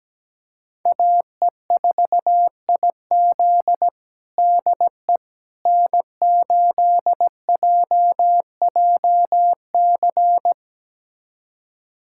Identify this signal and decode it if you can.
Morse code, heard as a keyed tone, about 17 wpm, 700 Hz. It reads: AE4IZ DE N8JJC
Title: USA Callsign DE Callsign - 5 Characters - 17wpm